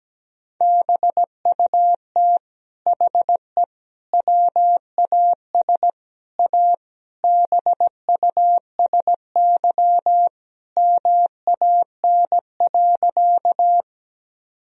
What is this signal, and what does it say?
Morse code, heard as a keyed tone, about 17 wpm, 700 Hz. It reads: BUT HE WAS A BUSY MAN.
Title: Sentences from Top 1000 Words - Review Part 3 - 17wpm